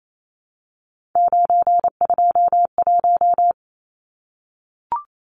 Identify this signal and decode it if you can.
Morse code, heard as a keyed tone, about 28 wpm, 700 Hz. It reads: 921
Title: Numbers - 3-Digit 28wpm